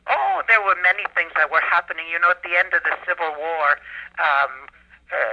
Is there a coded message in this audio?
Yes: Emergency Alert System (SAME) tones from 1.1 to 4.1 s, heard as harsh buzzing bursts, with speech.